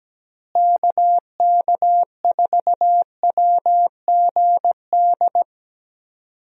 Morse code, heard as a keyed tone, about 17 wpm, 700 Hz. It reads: KK4WGD
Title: USA Call Signs - 6 Characters - Repeated 3x - 17wpm